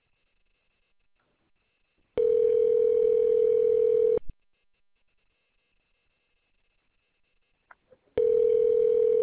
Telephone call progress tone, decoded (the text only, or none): ringback tone